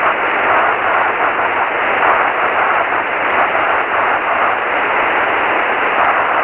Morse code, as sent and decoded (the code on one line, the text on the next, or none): none